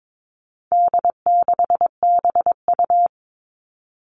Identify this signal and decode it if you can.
Morse code, heard as a keyed tone, about 22 wpm, 700 Hz. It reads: D6BU
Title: All Cut Numbers - 4-Digit 22wpm